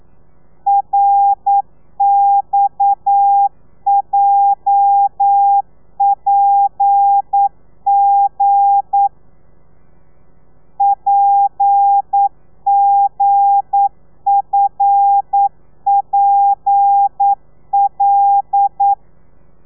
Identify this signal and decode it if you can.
Morse code, heard as a keyed tone, about 9 wpm, 800 Hz. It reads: RXJPG PGFPL
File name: telegrafia / i2tfj / Lez8 40